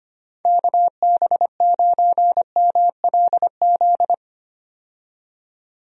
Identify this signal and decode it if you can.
Morse code, heard as a keyed tone, about 25 wpm, 700 Hz. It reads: KB9MLZ
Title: USA Call Signs - 6 Characters - 25wpm